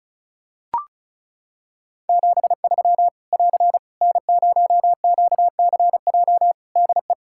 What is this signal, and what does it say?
Morse code, heard as a keyed tone, about 36 wpm, 700 Hz. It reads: 73 <AR> N0QCJ DE
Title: K1USN SST QSO Practice 35wpm